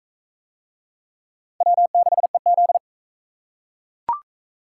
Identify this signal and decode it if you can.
Morse code, heard as a keyed tone, about 41 wpm, 700 Hz. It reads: W6EZ